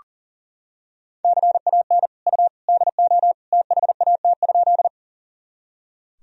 Morse code, heard as a keyed tone, about 40 wpm, 700 Hz. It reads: CAN U DO THAT?
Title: Sentences from Top 100 Words and Abbreviations 40wpm